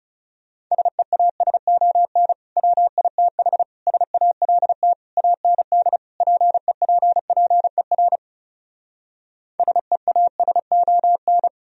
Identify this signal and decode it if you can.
Morse code, heard as a keyed tone, about 40 wpm, 700 Hz. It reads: SEASON WITH SALT AND PEPPER SEASON